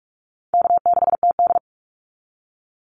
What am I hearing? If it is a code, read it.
Morse code, heard as a keyed tone, about 44 wpm, 700 Hz. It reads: K6TD
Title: Easy Call Signs Repeated 2x - 45wpm